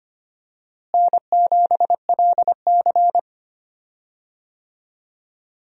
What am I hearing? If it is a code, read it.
Morse code, heard as a keyed tone, about 25 wpm, 700 Hz. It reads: N7LC
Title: USA Call Signs - 4 Characters - Repeated 2x - 25wpm